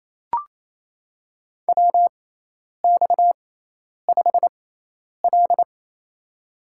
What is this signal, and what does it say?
Morse code, heard as a keyed tone, about 28 wpm, spaced as for 10 wpm, 700 Hz. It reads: WX5L
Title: USA Call Signs - 4 Characters - 6x Character Spacing - 28wpm